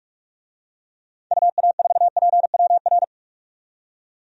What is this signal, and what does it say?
Morse code, heard as a keyed tone, about 46 wpm, 700 Hz. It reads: UA4PWR
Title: DXCC-100 Call Signs - 6 Characters - 45wpm